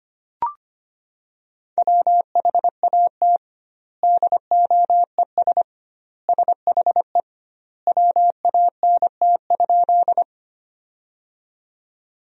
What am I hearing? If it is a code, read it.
Morse code, heard as a keyed tone, about 25 wpm, 700 Hz. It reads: WHAT DOES SHE WANT?
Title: 2x Word-Spacing - Sentences from Top 200 Words - 25wpm